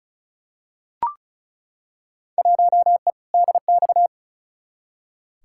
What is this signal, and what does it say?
Morse code, heard as a keyed tone, about 36 wpm, 700 Hz. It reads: 1E DX
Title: Field Day Exchanges Encore - 35wpm